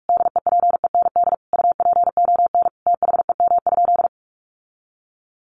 Morse code, heard as a keyed tone, about 44 wpm, 700 Hz. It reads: DEPEND UPON THEM?
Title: Sentences from Top 1300 Words 45wpm